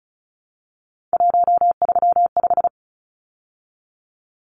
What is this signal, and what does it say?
Morse code, heard as a keyed tone, about 34 wpm, 700 Hz. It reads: 135
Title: Numbers - 3-Digit 35wpm